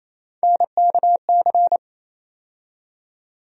Morse code, heard as a keyed tone, about 27 wpm, 700 Hz. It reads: NKC
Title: USA Call Sign Postfixes - 3 Letters 28wpm